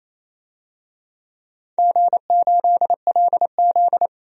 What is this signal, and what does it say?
Morse code, heard as a keyed tone, about 28 wpm, 700 Hz. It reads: G8LZ
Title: DXCC-100 Call Signs - 4 Characters - 28wpm